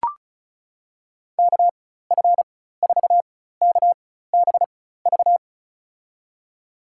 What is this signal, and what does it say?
Morse code, heard as a keyed tone, about 36 wpm, spaced as for 16 wpm, 700 Hz. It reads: KF4KBV
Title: USA Call Signs - 6 Characters - 4x Character Spacing - 35wpm